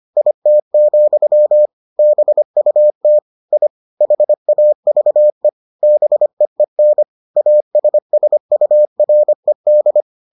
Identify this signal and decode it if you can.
Morse code, heard as a keyed tone, about 25 wpm, 600 Hz. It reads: IT, BUT I HAVE BEEN ASSURED